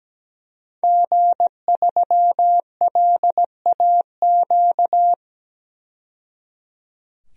Morse code, heard as a keyed tone, about 17 wpm, 700 Hz. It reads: G3LAQ
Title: DXCC-100 Call Signs - 5 Characters - 17wpm